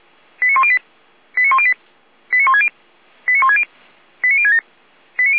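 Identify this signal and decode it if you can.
Five-tone selective call (ZVEI), heard as quick runs of five tones, tones of about 70 ms. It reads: 89108 89108 89160 89160 89067
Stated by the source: Five-tone ZVEI selective call received on an SDR, 455.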